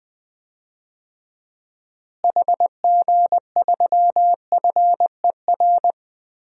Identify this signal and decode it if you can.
Morse code, heard as a keyed tone, about 20 wpm, 700 Hz. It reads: HG3FER
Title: DXCC-100 Call Signs - 6 Characters - Repeated 2x - 20wpm